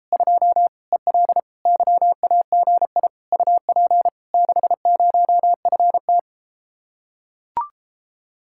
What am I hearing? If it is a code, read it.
Morse code, heard as a keyed tone, about 33 wpm, 700 Hz. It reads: 2 EL YAGI UP 60FT